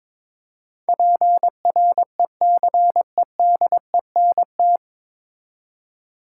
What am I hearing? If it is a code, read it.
Morse code, heard as a keyed tone, about 22 wpm, 700 Hz. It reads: PRECEDENT
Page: Morse Code Ninja practice 22wpm